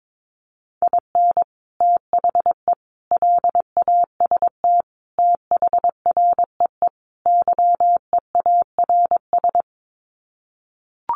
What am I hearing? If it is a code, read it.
Morse code, heard as a keyed tone, about 22 wpm, 700 Hz. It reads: IN THE LAST THREE YEARS
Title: Sets of 5 Words Encore2 22wpm